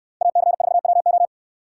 Morse code, heard as a keyed tone, about 69 wpm, 690 Hz. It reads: AC4WO